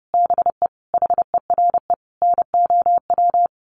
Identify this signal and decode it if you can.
Morse code, heard as a keyed tone, about 30 wpm, 700 Hz. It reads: BE HERE NOW
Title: A Sound Of Thunder 30wpm